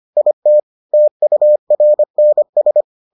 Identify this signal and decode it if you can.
Morse code, heard as a keyed tone, about 25 wpm, 600 Hz. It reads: IT TURNS